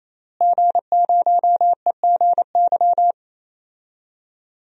Morse code, heard as a keyed tone, about 28 wpm, 700 Hz. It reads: G0EGY
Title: DXCC-100 Call Signs - 5 Characters - Repeated 3x - 28wpm